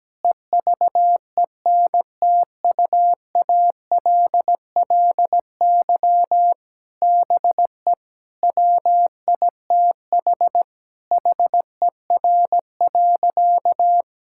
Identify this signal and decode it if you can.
Morse code, heard as a keyed tone, about 17 wpm, 700 Hz. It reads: EVENTUALLY BE WITH HER.
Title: Sentences from Top 1000 Words - Review Part 6 - 17wpm